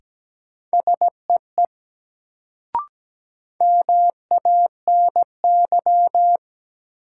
Morse code, heard as a keyed tone, about 17 wpm, 700 Hz. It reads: SEE MANY
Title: Sets of 2 Words - New Word Drill - 17wpm